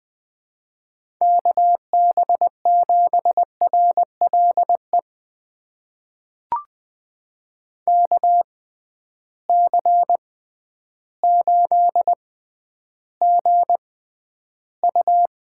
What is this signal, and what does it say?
Morse code, heard as a keyed tone, about 20 wpm, spaced as for 11 wpm, 700 Hz. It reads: KB7RLE KC8GU
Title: USA Call Signs - 6 Characters - 6x Character Spacing - 20wpm